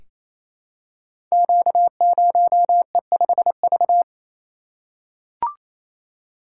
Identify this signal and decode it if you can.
Morse code, heard as a keyed tone, about 28 wpm, 700 Hz. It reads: Q0E5V
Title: Code Groups - 5 Letter:Numbers - 28wpm